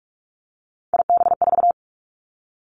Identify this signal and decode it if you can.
Morse code, heard as a keyed tone, about 46 wpm, 700 Hz. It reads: IB4
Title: Three Challenging Letter-Number Combinations 45wpm